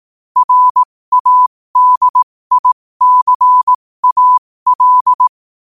Morse code, heard as a keyed tone, about 18 wpm, spaced as for 15 wpm, 1000 Hz. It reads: RADICAL
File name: qotdfiles / 15wpm